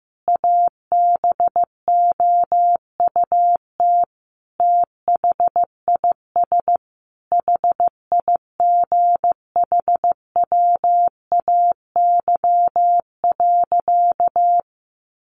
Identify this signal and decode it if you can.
Morse code, heard as a keyed tone, about 15 wpm, 700 Hz. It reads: ABOUT THIS HIGHWAY.
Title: Sentences from Top 1400 Words 15wpm